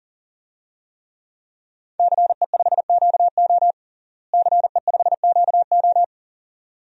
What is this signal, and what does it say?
Morse code, heard as a keyed tone, about 37 wpm, 700 Hz. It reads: CE5QO CE5QO
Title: DXCC-100 Call Signs - 5 Characters - Repeated 2x - 40wpm